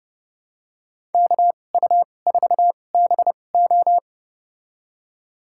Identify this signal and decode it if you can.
Morse code, heard as a keyed tone, about 30 wpm, spaced as for 22 wpm, 700 Hz. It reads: KU4BO
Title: USA Call Signs - 5 Characters - 2x Character Spacing - 30wpm